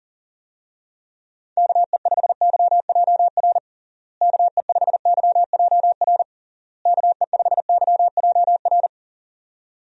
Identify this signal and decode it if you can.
Morse code, heard as a keyed tone, about 40 wpm, 700 Hz. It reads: KE5YJR KE5YJR KE5YJR
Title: USA Call Signs - 6 Characters - Repeated 3x - 40wpm